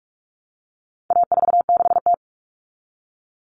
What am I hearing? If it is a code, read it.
Morse code, heard as a keyed tone, about 44 wpm, 700 Hz. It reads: A46T
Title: Common Cut Numbers - 4-Digit 45wpm